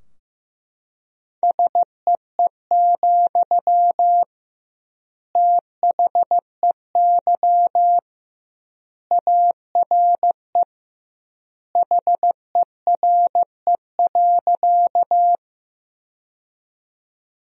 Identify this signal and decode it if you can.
Morse code, heard as a keyed tone, about 15 wpm, 700 Hz. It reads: SEE, THEY ARE HERE.
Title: 2x Word-Spacing - Sentences from Top 200 Words - 15wpm